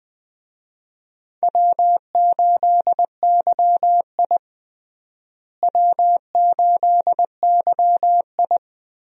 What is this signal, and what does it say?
Morse code, heard as a keyed tone, about 20 wpm, 700 Hz. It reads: W8YI W8YI
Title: USA Call Signs - 4 Characters - Repeated 3x - 20wpm